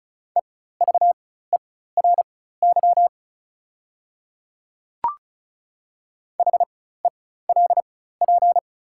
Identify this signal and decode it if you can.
Morse code, heard as a keyed tone, about 34 wpm, spaced as for 16 wpm, 690 Hz. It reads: EVERY HELP
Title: 4x Character Spacing - Top 200 Words - 35wpm